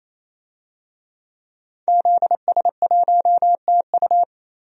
Morse code, heard as a keyed tone, about 27 wpm, 700 Hz. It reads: ZS1TU
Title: DXCC-100 Call Signs - 5 Characters - 28wpm